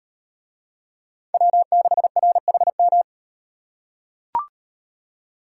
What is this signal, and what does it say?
Morse code, heard as a keyed tone, about 36 wpm, 700 Hz. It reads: W6RHM